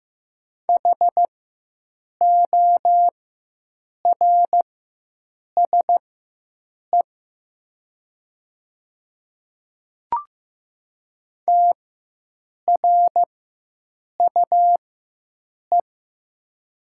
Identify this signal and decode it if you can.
Morse code, heard as a keyed tone, about 15 wpm, spaced as for 7 wpm, 700 Hz. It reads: HORSE TRUE
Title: 4x Character Spacing - Top 500 Words - 15wpm